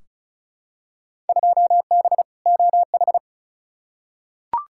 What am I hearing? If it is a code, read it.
Morse code, heard as a keyed tone, about 34 wpm, 700 Hz. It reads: 2B OH
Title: Field Day Exchanges - 35wpm